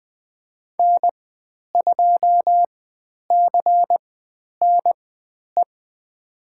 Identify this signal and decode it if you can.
Morse code, heard as a keyed tone, about 20 wpm, spaced as for 10 wpm, 700 Hz. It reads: N2CNE
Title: 102 Lesson 17g - Letter u - Call Signs 20wpm